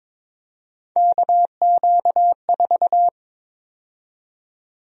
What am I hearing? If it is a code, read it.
Morse code, heard as a keyed tone, about 22 wpm, 700 Hz. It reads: KQ4
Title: USA Call Sign Prefixes - 22wpm